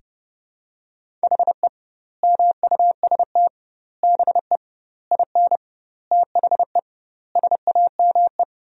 Morse code, heard as a keyed tone, about 30 wpm, 700 Hz. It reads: HE MUST BE IN THE SAME
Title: 2x Word-Spacing - Sentences from Top 500 Words - 30wpm